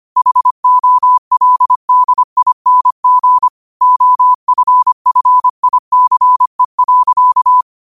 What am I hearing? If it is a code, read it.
Morse code, heard as a keyed tone, about 25 wpm, 1000 Hz. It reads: HOLDING OFFICE.
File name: qotdfiles / 25wpm